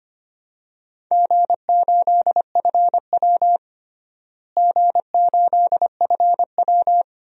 Morse code, heard as a keyed tone, about 25 wpm, 700 Hz. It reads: G8FW G8FW
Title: DXCC-100 Call Signs - 4 Characters - Repeated 3x - 25wpm